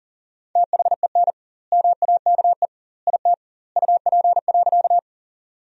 Morse code, heard as a keyed tone, about 40 wpm, 700 Hz. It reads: THEN MAKE IT UP.
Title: 2x Word-Spacing - Sentences from Top 100 Words - 40wpm